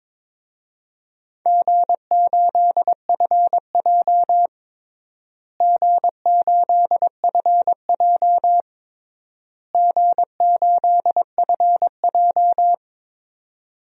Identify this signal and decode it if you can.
Morse code, heard as a keyed tone, about 22 wpm, 700 Hz. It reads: G8FJ G8FJ G8FJ
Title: DXCC-100 Call Signs - 4 Characters - Repeated 3x - 22wpm